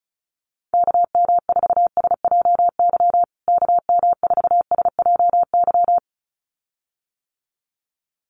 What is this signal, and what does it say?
Morse code, heard as a keyed tone, about 34 wpm, 700 Hz. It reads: KM4SJY KM4SJY
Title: USA Call Signs Repeated 2x 35wpm